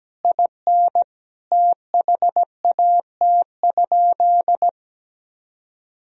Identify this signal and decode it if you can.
Morse code, heard as a keyed tone, about 17 wpm, 700 Hz. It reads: IN THAT?
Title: Sentences from Top 1000 Words - Review Part 4 - 17wpm